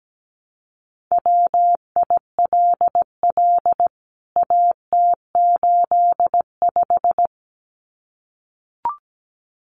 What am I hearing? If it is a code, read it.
Morse code, heard as a keyed tone, about 17 wpm, 700 Hz. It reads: WILL AT85
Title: CWT Exchange 17wpm